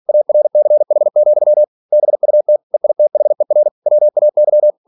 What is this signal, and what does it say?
Morse code, heard as a keyed tone, about 46 wpm, 600 Hz. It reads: ARCH, BUT EITHER WAY